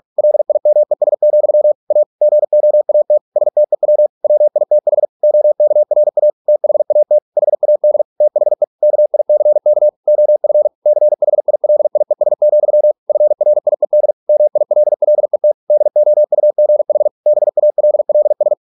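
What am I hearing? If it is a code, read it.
Morse code, heard as a keyed tone, about 46 wpm, 600 Hz. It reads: LIMES, A GOAT STEW WITH OKRA THAT HAD THE KICK OF CHILIES, FRIED MILLET DOUGH BALLS
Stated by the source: Morse practice at 45 wpm, the novel Never by Ken Follett in English, from CW med Gnister.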